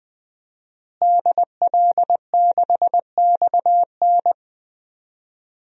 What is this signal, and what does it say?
Morse code, heard as a keyed tone, about 20 wpm, 700 Hz. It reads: DL6XN
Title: DXCC-100 Call Signs - 5 Characters - Repeated 2x - 20wpm